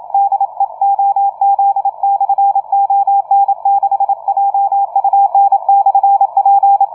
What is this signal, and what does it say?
Morse code, heard as a keyed tone, about 28 wpm, 800 Hz, with radio noise. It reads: DE OZ/ON6JUN/P
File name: on6jun / OZ ON6JUN-CW-30m